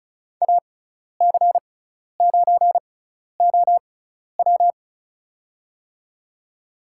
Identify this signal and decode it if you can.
Morse code, heard as a keyed tone, about 36 wpm, spaced as for 12 wpm, 700 Hz. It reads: AC9OW
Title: USA Call Signs - 6x Character Spacing - 35wpm